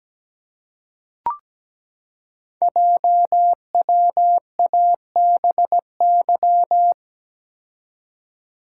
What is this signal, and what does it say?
Morse code, heard as a keyed tone, about 17 wpm, 700 Hz. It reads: JWABY
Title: Code Groups - 5 Letters - 17wpm